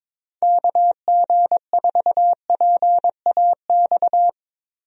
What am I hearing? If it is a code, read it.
Morse code, heard as a keyed tone, about 22 wpm, 700 Hz. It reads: KG4PAX
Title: USA Call Signs - 6 Characters - Repeated 2x - 22wpm